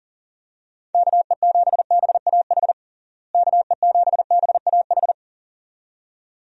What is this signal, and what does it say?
Morse code, heard as a keyed tone, about 40 wpm, 700 Hz. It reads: KE7BAH KE7BAH
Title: USA Call Signs - 6 Characters - Repeated 2x - 40wpm